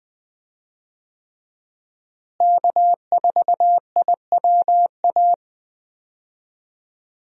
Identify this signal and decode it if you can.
Morse code, heard as a keyed tone, about 21 wpm, 700 Hz. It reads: K4IWA
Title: USA Call Signs - 5 Characters - Repeated 3x - 20wpm